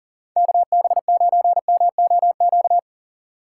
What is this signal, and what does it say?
Morse code, heard as a keyed tone, about 40 wpm, 700 Hz. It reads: KB9MOQ
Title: USA Call Signs - 6 Characters - 2x Character Spacing - 40wpm